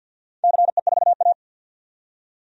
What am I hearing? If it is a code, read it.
Morse code, heard as a keyed tone, about 48 wpm, 700 Hz. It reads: CE4A
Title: DXCC-100 Call Signs - 4 Characters - 50wpm